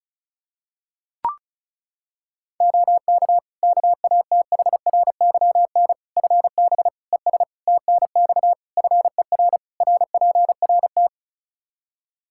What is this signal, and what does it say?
Morse code, heard as a keyed tone, about 36 wpm, 700 Hz. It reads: OK KATHRYN FB ES TNX FER RPRT
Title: K1USN SST QSO Practice 35wpm